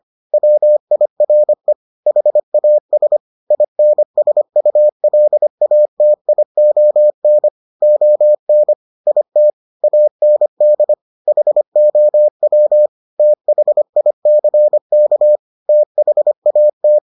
Morse code, heard as a keyed tone, about 25 wpm, 600 Hz. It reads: WIRE HAS INSULATION ON IT AND HOW THICK THAT